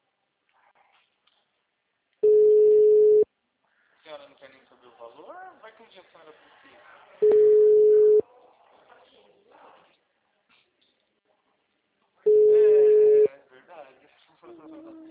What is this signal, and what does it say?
Ringback tone (Europe, 425 Hz), with speech.